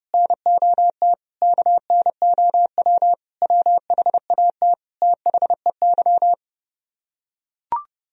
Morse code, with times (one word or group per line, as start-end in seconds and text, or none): 0.1-1.1 NOT
1.4-3.1 KNOW
3.4-4.7 WHAT
5.0-6.3 THEY